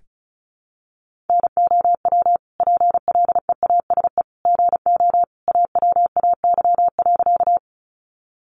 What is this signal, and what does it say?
Morse code, heard as a keyed tone, about 34 wpm, 700 Hz. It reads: NOW PLEASE GO AWAY.
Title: Sentences from Top 1500 Words 35wpm